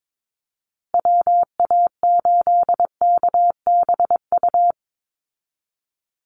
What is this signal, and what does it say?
Morse code, heard as a keyed tone, about 22 wpm, 700 Hz. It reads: WA8KBU
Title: Call Signs Repeated 2x - 22wpm